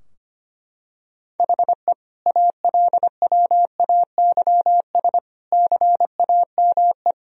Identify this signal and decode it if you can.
Morse code, heard as a keyed tone, about 25 wpm, 700 Hz. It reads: HE ALWAYS CAME
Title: Sentences from Top 1000 Words - Review Part 4 - 25wpm